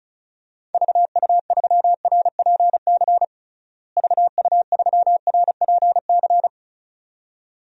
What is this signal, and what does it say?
Morse code, heard as a keyed tone, about 36 wpm, 700 Hz. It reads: VU3RPC VU3RPC
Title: DXCC-100 Call Signs - 6 Characters - Repeated 2x - 35wpm